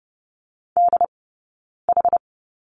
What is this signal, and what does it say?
Morse code, heard as a keyed tone, about 30 wpm, spaced as for 9 wpm, 690 Hz. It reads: DH
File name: Lesson 17d - Letter u - Two Characters 30wpm